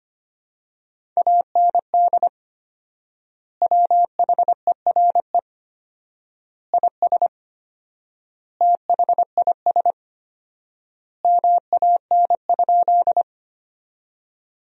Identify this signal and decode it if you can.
Morse code, heard as a keyed tone, about 25 wpm, 700 Hz. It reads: AND WHERE IS THIS MAN?
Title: 4x Word-Spacing - Sentences from Top 200 Words - 25wpm